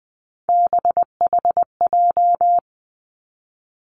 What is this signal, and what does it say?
Morse code, heard as a keyed tone, about 20 wpm, 700 Hz. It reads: BHJ
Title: Three Challenging Letter-Number Combinations 20wpm